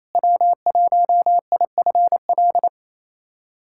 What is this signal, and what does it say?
Morse code, heard as a keyed tone, about 28 wpm, 700 Hz. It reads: W1IFL
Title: USA Callsign DE Callsign - 5 Characters - 28wpm